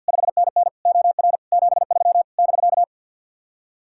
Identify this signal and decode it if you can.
Morse code, heard as a keyed tone, about 64 wpm, 700 Hz. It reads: 5NN OR 73 <BK>